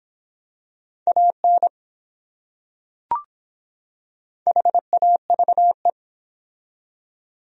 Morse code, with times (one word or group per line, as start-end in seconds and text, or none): 1.1-1.7 AN
4.5-5.9 HAVE